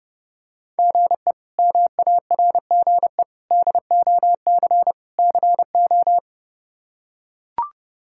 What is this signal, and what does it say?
Morse code, heard as a keyed tone, about 30 wpm, 700 Hz. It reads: GE MARGE DOC CO
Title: K1USN SST QSO Practice - 30wpm